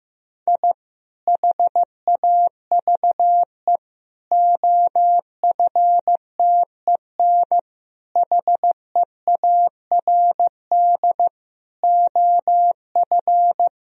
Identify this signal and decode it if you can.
Morse code, heard as a keyed tone, about 15 wpm, 700 Hz. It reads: I HAVE OFTEN HEARD OF
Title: Sentences from Top 1000 Words - Review Part 2 - 15wpm